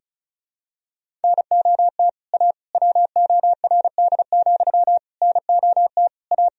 Morse code, heard as a keyed tone, about 34 wpm, 690 Hz. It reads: NOT A WORD, NOT A
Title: Sentences from Top 1000 Words - Review Part 1 - 35wpm